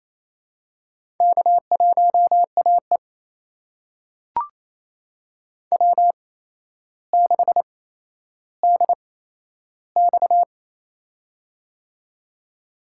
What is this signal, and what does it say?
Morse code, heard as a keyed tone, about 28 wpm, spaced as for 12 wpm, 700 Hz. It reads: K1AE W6DX
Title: USA Call Signs - 4 Characters - 8x Character Spacing - 28wpm